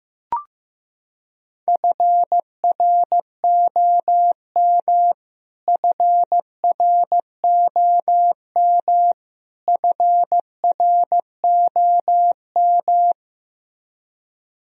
Morse code, heard as a keyed tone, about 15 wpm, 700 Hz. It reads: FROM FROM FROM
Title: Top 100 Words Repeated 15wpm